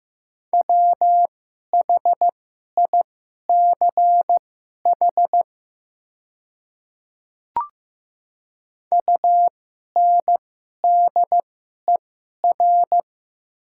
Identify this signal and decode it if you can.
Morse code, heard as a keyed tone, about 15 wpm, spaced as for 11 wpm, 700 Hz. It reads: WHICH UNDER
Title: 2x Character Spacing - Top 300 Words - 15wpm